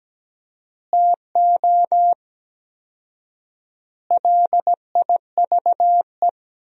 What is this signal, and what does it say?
Morse code, heard as a keyed tone, about 17 wpm, 700 Hz. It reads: TO LIVE
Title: 4x Word-Spacing - Sentences from Top 200 Words - 17wpm